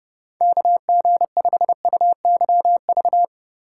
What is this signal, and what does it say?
Morse code, heard as a keyed tone, about 28 wpm, 700 Hz. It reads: KG5UYV